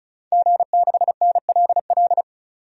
Morse code, heard as a keyed tone, about 36 wpm, 700 Hz. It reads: G6NLL
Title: DXCC-100 Call Signs - 5 Characters - 35wpm